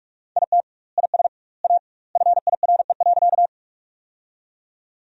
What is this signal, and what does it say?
Morse code, heard as a keyed tone, about 46 wpm, 700 Hz. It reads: IT IS A FIRE.
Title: 2x Word-Spacing - Sentences from Top 400 Words - 45wpm